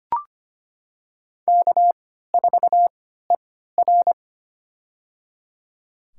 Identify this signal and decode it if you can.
Morse code, heard as a keyed tone, about 25 wpm, spaced as for 14 wpm, 700 Hz. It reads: K4ER
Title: USA Call Signs - 3x Character Spacing - 25wpm